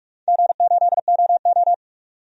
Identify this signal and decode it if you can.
Morse code, heard as a keyed tone, about 41 wpm, 700 Hz. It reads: G8OO
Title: DXCC-100 Call Signs - 4 Characters - 45wpm